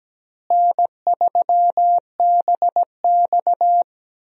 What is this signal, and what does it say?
Morse code, heard as a keyed tone, about 17 wpm, 700 Hz. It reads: N3BX
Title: USA Call Signs - 4 Characters - Repeated 2x - 17wpm